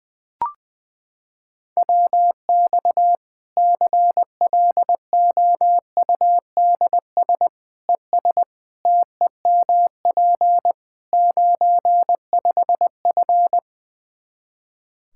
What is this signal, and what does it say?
Morse code, heard as a keyed tone, about 21 wpm, 700 Hz. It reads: WX CLOUDS ES TEMP 95F
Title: Practice QSO - Exchange 2 - Info - 20wpm